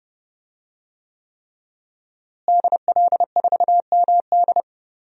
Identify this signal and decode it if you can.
Morse code, heard as a keyed tone, about 30 wpm, 700 Hz. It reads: DL4MD
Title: DXCC-100 Call Signs - 5 Characters - Repeated 3x - 30wpm